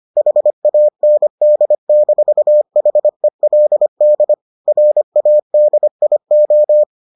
Morse code, heard as a keyed tone, about 25 wpm, 600 Hz. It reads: HAND-HELD RADIO